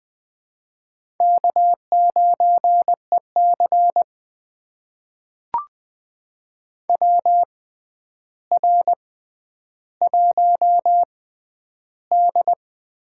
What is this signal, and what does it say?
Morse code, heard as a keyed tone, about 21 wpm, spaced as for 10 wpm, 700 Hz. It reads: K9EC WR1D
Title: USA Call Signs - 4 Characters - 6x Character Spacing - 20wpm